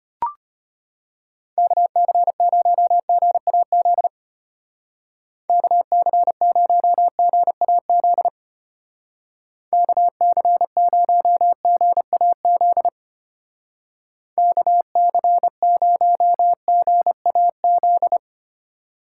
Morse code, heard as a keyed tone, about 33 wpm, 700 Hz. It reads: KC0GAZ KC0GAZ KC0GAZ KC0GAZ